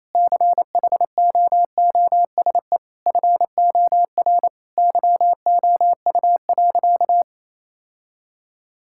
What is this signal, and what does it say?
Morse code, heard as a keyed tone, about 27 wpm, 700 Hz. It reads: CHOOSE FOR YOU.
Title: Sentences from Top 1000 Words - Review Part 6 - 28wpm